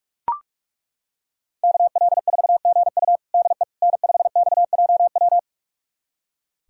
Morse code, heard as a keyed tone, about 42 wpm, 700 Hz. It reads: KL4GU DE N5XJW
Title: USA Callsign DE Callsign - 5 Characters - 45wpm